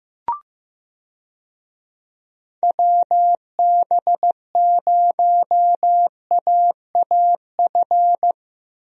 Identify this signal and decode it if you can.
Morse code, heard as a keyed tone, about 15 wpm, 700 Hz. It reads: WB0AAF
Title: USA Call Signs - 6 Characters - Repeated 3x - 15wpm